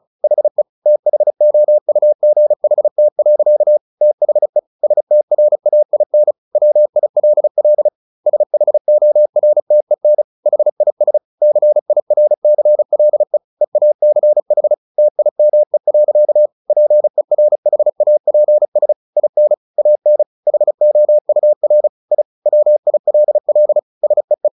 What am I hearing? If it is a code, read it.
Morse code, heard as a keyed tone, about 34 wpm, 600 Hz. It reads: HE THOUGHT. THE STRAIN WILL SHORTEN HIS CIRCLE EACH TIME. PERHAPS IN AN HOUR I WILL SEE
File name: the old man and the sea 2 35wpm